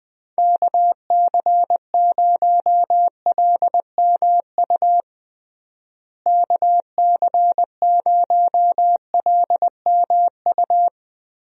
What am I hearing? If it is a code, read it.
Morse code, heard as a keyed tone, about 20 wpm, 700 Hz. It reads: KC0LMU KC0LMU
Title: USA Call Signs - 6 Characters - Repeated 3x - 20wpm